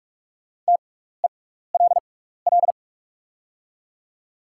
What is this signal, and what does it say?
Morse code, heard as a keyed tone, about 44 wpm, spaced as for 15 wpm, 690 Hz. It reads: TELL
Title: 6x Character Spacing - Top 300 Words - 45wpm